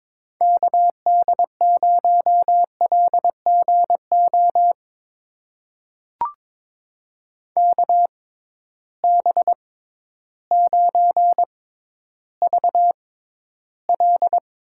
Morse code, heard as a keyed tone, about 22 wpm, spaced as for 12 wpm, 700 Hz. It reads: KD0LGO KB9VL
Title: USA Call Signs - 6 Characters - 6x Character Spacing - 22wpm